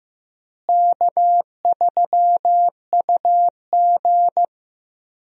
Morse code, heard as a keyed tone, about 15 wpm, 700 Hz. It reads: K3UG
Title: USA Call Signs - 4 Characters - Repeated 3x - 15wpm